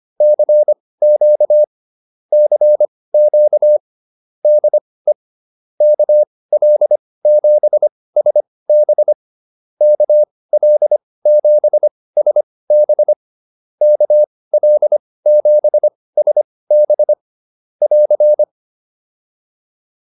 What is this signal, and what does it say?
Morse code, heard as a keyed tone, about 25 wpm, spaced as for 18 wpm, 600 Hz. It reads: CQ CQ DE KL7SB KL7SB KL7SB <AR>